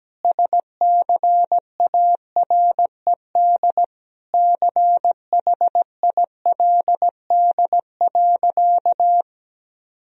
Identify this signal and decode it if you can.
Morse code, heard as a keyed tone, about 17 wpm, 700 Hz. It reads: SCARED CHILD.
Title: Sentences from Top 1000 Words - Review Part 4 - 17wpm